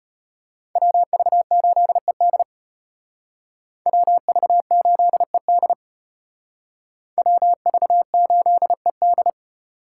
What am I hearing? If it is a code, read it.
Morse code, heard as a keyed tone, about 32 wpm, 700 Hz. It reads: WV8ED WV8ED WV8ED